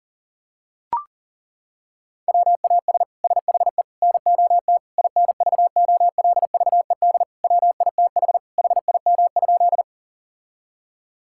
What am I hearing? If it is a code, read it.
Morse code, heard as a keyed tone, about 40 wpm, 700 Hz. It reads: WAS SHE NOT INVOLVED WITH HIM?
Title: Sentences from Top 1000 Words - Review Part 5 - 40wpm